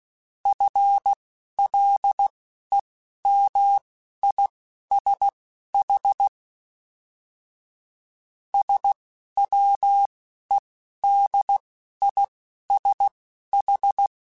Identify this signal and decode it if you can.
Morse code, heard as a keyed tone, about 16 wpm, spaced as for 11 wpm, 800 Hz. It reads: FLEMISH SWEDISH
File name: General / General-mp3s / ISH @10